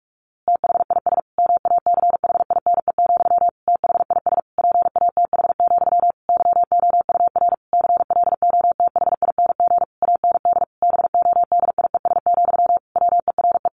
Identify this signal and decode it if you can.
Morse code, heard as a keyed tone, about 44 wpm, 700 Hz. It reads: THIS MACHINE, THIS PATH, YOUR CLOTHING AND BODIES, WERE
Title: A Sound Of Thunder 45wpm